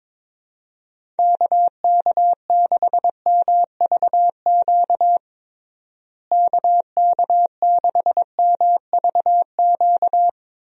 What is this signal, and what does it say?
Morse code, heard as a keyed tone, about 22 wpm, 700 Hz. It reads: KK6MVQ KK6MVQ
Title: USA Call Signs - 6 Characters - Repeated 3x - 22wpm